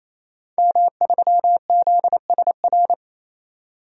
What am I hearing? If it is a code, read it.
Morse code, heard as a keyed tone, about 28 wpm, 700 Hz. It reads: M3ZSR
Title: DXCC-100 Call Signs - 5 Characters - 28wpm